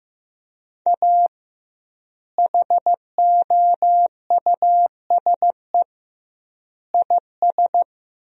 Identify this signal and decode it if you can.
Morse code, heard as a keyed tone, about 16 wpm, 700 Hz. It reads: A HOUSE IS
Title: 2x Word-Spacing - Sentences from Top 400 Words - 15wpm